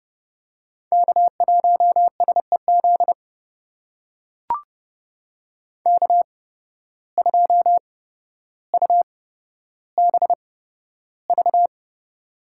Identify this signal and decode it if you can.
Morse code, heard as a keyed tone, about 32 wpm, spaced as for 13 wpm, 700 Hz. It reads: K1SEZ K2UBV
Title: USA Call Signs - 5 Characters - 8x Character Spacing - 30wpm